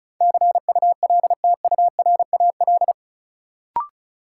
Morse code, heard as a keyed tone, about 36 wpm, 700 Hz. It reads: CULTURAL
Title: Sentences from Top 1500 Words - New Word Drill Alt - 35wpm